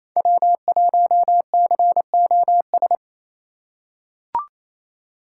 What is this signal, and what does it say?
Morse code, heard as a keyed tone, about 28 wpm, 700 Hz. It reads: W1COS
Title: USA Call Signs - 5 Characters - 4x Character Spacing - 28wpm